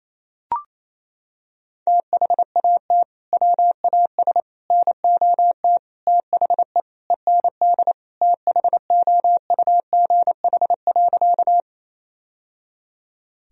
Morse code, heard as a keyed tone, about 28 wpm, 700 Hz. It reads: THAT WAS NOT THE END THOUGH.
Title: Sentences from Top 1000 Words - Review Part 2 - 28wpm